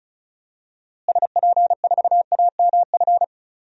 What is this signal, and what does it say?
Morse code, heard as a keyed tone, about 36 wpm, 700 Hz. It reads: SP4AMF
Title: DXCC-100 Call Signs - 6 Characters - Repeated 2x - 35wpm